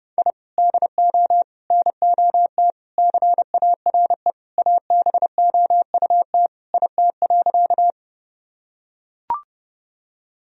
Morse code, heard as a keyed tone, about 30 wpm, 700 Hz. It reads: I DO NOT CARE ABOUT IT.
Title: Sentences from Top 1000 Words - Review Part 1 - 30wpm